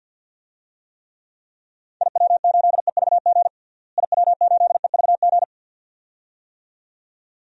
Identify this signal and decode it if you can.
Morse code, heard as a keyed tone, about 46 wpm, 700 Hz. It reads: IW8EVG IW8EVG
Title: DXCC-100 Call Signs - 6 Characters - Repeated 2x - 50wpm